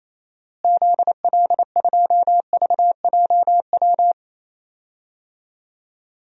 Morse code, heard as a keyed tone, about 28 wpm, 700 Hz. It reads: ZL2VJW
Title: DXCC-100 Call Signs - 6 Characters - Repeated 3x - 28wpm